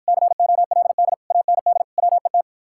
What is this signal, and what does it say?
Morse code, heard as a keyed tone, about 54 wpm, 700 Hz. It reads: COLD AND WET